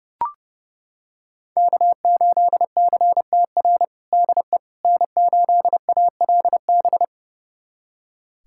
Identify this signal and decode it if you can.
Morse code, heard as a keyed tone, about 30 wpm, 700 Hz. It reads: K8CTR DE N8ALB
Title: USA Callsign DE Callsign - 5 Characters - 30wpm